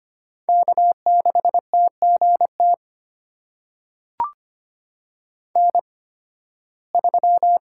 Morse code, heard as a keyed tone, about 25 wpm, spaced as for 16 wpm, 700 Hz. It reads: K6TGT N3
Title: USA Call Signs - 5 Characters - 8x Character Spacing - 25wpm